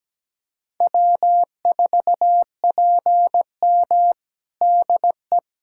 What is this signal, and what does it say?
Morse code, heard as a keyed tone, about 17 wpm, 700 Hz. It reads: W4PM DE
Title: Practice QSO - Exchange 4 - Ending - 17wpm